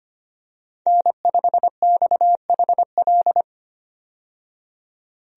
Morse code, heard as a keyed tone, about 25 wpm, 700 Hz. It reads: N5XHL
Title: USA Call Signs - 5 Characters - 25wpm